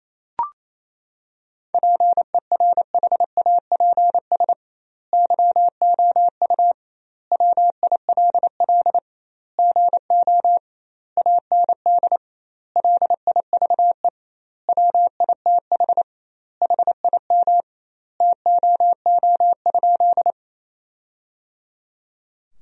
Morse code, heard as a keyed tone, about 28 wpm, 700 Hz. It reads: PERHAPS YOU WILL GO AND LIVE WITH HIM TOO?
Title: 2x Word-Spacing - Sentences from Top 500 Words - 28wpm